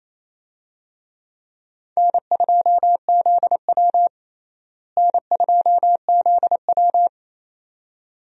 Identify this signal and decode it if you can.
Morse code, heard as a keyed tone, about 28 wpm, 700 Hz. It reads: N2ZW N2ZW
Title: USA Call Signs - 4 Characters - Repeated 2x - 28wpm